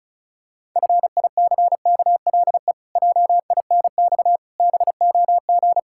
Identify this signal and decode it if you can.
Morse code, heard as a keyed tone, about 34 wpm, 700 Hz. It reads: FICKLE JINX BOG
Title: Pangram Challenge v2 - 35wpm